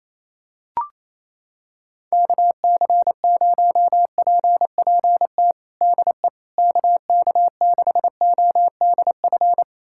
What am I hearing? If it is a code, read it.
Morse code, heard as a keyed tone, about 27 wpm, 700 Hz. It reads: KC0PPT DE KK6ODF
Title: USA Callsign DE Callsign 28wpm